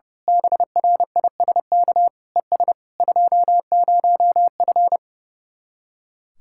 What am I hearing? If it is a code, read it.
Morse code, heard as a keyed tone, about 28 wpm, 700 Hz. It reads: BRISK ES 20F
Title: Speed-Racing Common QSO Phrases Encore - 30wpm